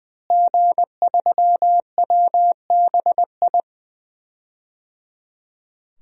Morse code, heard as a keyed tone, about 20 wpm, 700 Hz. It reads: G3WBI
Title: DXCC-100 Call Signs - 5 Characters - Repeated 3x - 20wpm